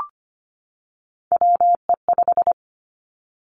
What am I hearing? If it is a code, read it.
Morse code, heard as a keyed tone, about 25 wpm, 700 Hz. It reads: WE5
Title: Three Challenging Letter-Number Combinations 25wpm